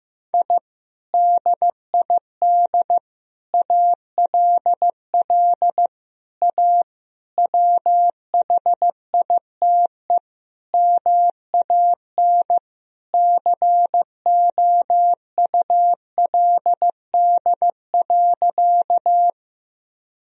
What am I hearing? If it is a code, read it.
Morse code, heard as a keyed tone, about 16 wpm, 700 Hz. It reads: I DID ALL A WHITE MAN COULD.
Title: Sentences from Top 1000 Words - Review Part 1 - 15wpm